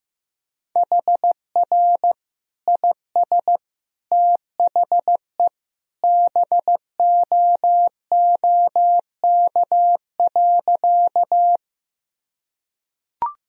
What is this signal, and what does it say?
Morse code, heard as a keyed tone, about 15 wpm, 700 Hz. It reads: HR IS THE BOOK.
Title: Sentences from Top 300 Words and Abbreviations 15wpm